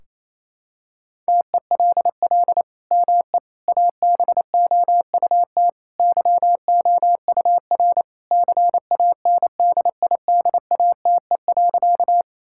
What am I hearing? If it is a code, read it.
Morse code, heard as a keyed tone, about 28 wpm, 700 Hz. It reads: TELL ME ABOUT YOUR CANDIDATE.
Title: Sentences from Top 1600 Words 28wpm